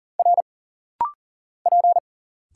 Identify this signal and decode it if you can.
Morse code, heard as a keyed tone, about 40 wpm, 700 Hz. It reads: R P
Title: Single Letter-Number Alt - Rapid-Fire 40wpm